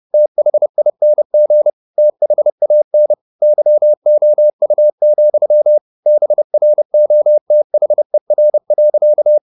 Morse code, heard as a keyed tone, about 30 wpm, 600 Hz. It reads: THING THAN YOU, BROTHER.